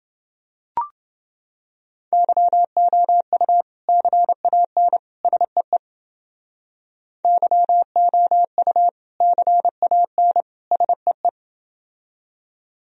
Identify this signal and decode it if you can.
Morse code, heard as a keyed tone, about 29 wpm, 700 Hz. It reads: YOU CAN SEE YOU CAN SEE